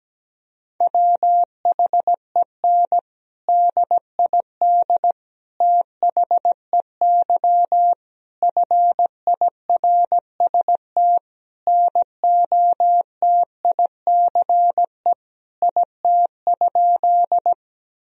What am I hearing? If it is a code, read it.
Morse code, heard as a keyed tone, about 17 wpm, 700 Hz. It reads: WHEN DID THEY FIRST NOTICE IT?
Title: Sentences from Top 1000 Words - Review Part 2 - 17wpm